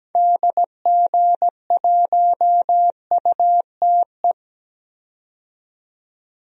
Morse code, heard as a keyed tone, about 17 wpm, 700 Hz. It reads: DG1UTE
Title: DXCC-100 Call Signs - 6 Characters - 17wpm